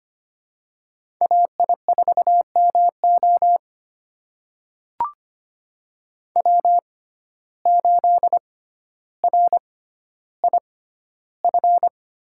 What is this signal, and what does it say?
Morse code, heard as a keyed tone, about 25 wpm, spaced as for 13 wpm, 700 Hz. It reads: AI4MO W8RIF
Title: USA Call Signs - 5 Characters - 6x Character Spacing - 25wpm